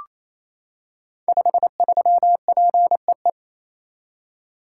Morse code, heard as a keyed tone, about 29 wpm, 700 Hz. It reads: 53PEE
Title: Code Groups - 5 Letter:Numbers - 28wpm